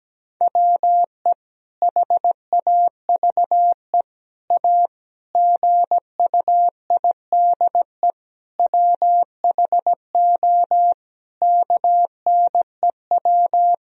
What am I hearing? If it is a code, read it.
Morse code, heard as a keyed tone, about 17 wpm, 700 Hz. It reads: WE HAVE A GUIDE WHO KNEW
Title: Sentences from Top 1000 Words - Review Part 3 - 17wpm